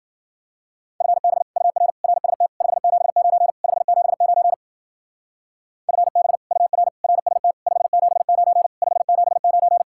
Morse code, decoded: FB UR RST 579 579 FB UR RST 579 579

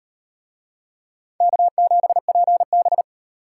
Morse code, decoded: K7PB

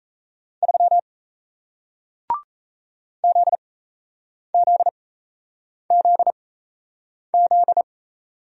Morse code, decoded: 3 ZZZZ

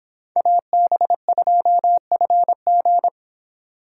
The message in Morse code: AB2FG

